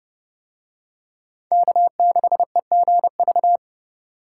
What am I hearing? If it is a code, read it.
Morse code, heard as a keyed tone, about 29 wpm, 700 Hz. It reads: K6EGV